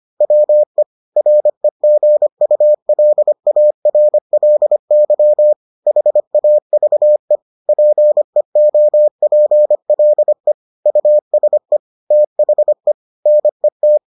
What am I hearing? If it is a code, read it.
Morse code, heard as a keyed tone, about 25 wpm, 600 Hz. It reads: WE REGULARLY HAVE PEOPLE USE THE NET